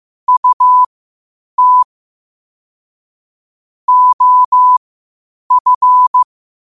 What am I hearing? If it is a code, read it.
Morse code, heard as a keyed tone, about 15 wpm, spaced as for 8 wpm, 1000 Hz. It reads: UT OF
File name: qotdfiles / 8wpm